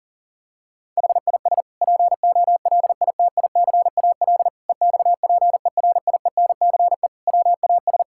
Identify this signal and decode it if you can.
Morse code, heard as a keyed tone, about 40 wpm, 700 Hz. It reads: HIS POLITICAL EXPERIENCE WAS
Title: Sentences from Top 1000 Words - Review Part 6 - 40wpm